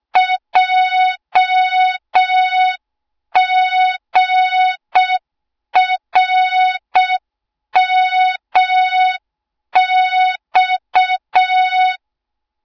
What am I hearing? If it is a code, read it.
Morse code, heard as a keyed tone, about 6 wpm, 3700 Hz. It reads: JGRMX
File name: Morze-HA2MN / hang / 20 GYAKORLAT